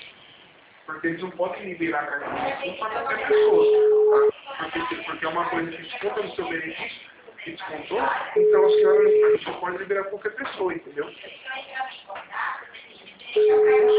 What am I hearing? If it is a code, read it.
Ringback tone (Europe, 425 Hz), with speech.
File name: out- -2008- - - .62226